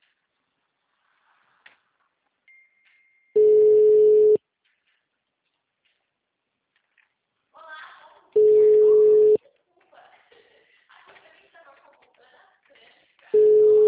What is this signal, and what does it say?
Ringback tone (Europe, 425 Hz), with speech.
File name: out- -2017- - - .1327